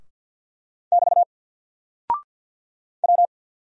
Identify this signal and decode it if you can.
Morse code, heard as a keyed tone, about 48 wpm, 700 Hz. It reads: <BT> W
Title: Single Character (Letter:Number:Punctuation:Prosign) Alt 50wpm